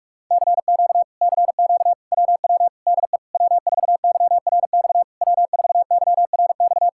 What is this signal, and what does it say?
Morse code, heard as a keyed tone, about 48 wpm, 700 Hz. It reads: CQ CQ WW DE W4YRX W4YRX